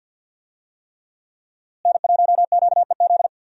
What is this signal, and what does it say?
Morse code, heard as a keyed tone, about 50 wpm, 700 Hz. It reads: N1QEZ
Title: USA Call Signs - 5 Characters - Repeated 3x - 50wpm